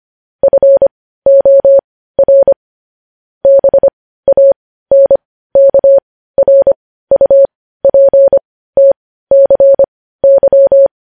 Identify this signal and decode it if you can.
Morse code, heard as a keyed tone, about 25 wpm, spaced as for 15 wpm, 550 Hz. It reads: FOR BANKRUPTCY
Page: news-headlines-in-morse-code-at-15-wpm